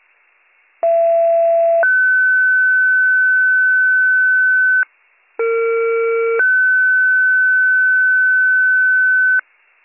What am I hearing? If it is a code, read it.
Two-tone page, heard as a one-second tone, then a three-second tone, about 670 then 1500 Hz, about 460 then 1500 Hz. It is heard from 0.8 to 9.4 s.